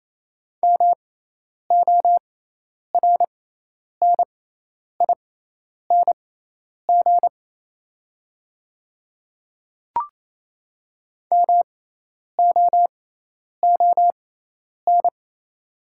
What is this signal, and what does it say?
Morse code, heard as a keyed tone, about 28 wpm, spaced as for 10 wpm, 700 Hz. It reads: MORNING MOON
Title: 6x Character Spacing - Top 500 Words - 28wpm